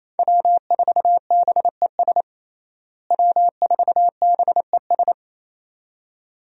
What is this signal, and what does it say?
Morse code, heard as a keyed tone, about 27 wpm, 700 Hz. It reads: W4BES W4BES
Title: USA Call Signs - 5 Characters - Repeated 2x - 28wpm